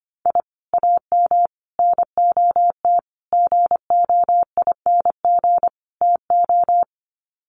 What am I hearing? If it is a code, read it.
Morse code, heard as a keyed tone, about 25 wpm, 690 Hz. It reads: I AM NOT GOING TO
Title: Sets of 5 Words Encore 25wpm